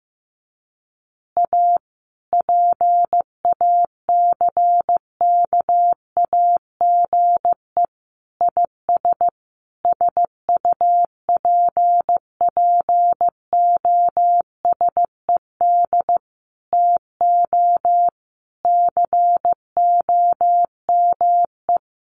Morse code, heard as a keyed tone, about 15 wpm, 700 Hz. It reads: A PACKAGE IS SUPPOSED TO COME
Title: Additional Easy Phrases 15wpm